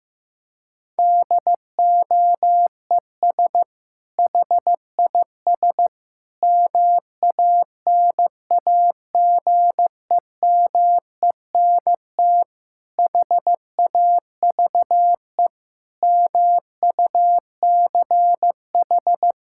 Morse code, heard as a keyed tone, about 16 wpm, 700 Hz. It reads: DOES HIS MANAGEMENT HAVE MUCH
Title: Sentences from Top 1000 Words - Review Part 6 - 15wpm